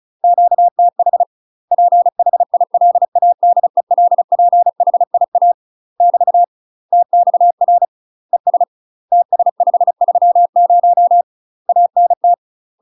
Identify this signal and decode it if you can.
Morse code, heard as a keyed tone, about 36 wpm, 700 Hz. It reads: QTH PHILADELPHIA <BT> TXR ES TS530 ANT
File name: UK MASTER QSOs TRAINING / 35WPM / UK MASTER QSOs 0036